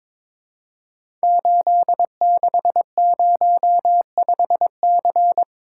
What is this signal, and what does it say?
Morse code, heard as a keyed tone, about 22 wpm, 690 Hz. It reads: 8605C
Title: Code Groups - 5 Letter:Numbers - 22wpm